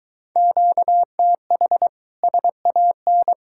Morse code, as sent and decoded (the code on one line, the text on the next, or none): --.- - .... / ... .- -.
QTH SAN